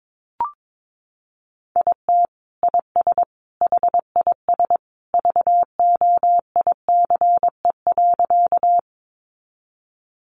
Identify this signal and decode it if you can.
Morse code, heard as a keyed tone, about 22 wpm, 700 Hz. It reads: IT IS HIS VOICE.
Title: Sentences from Top 500 Words 22wpm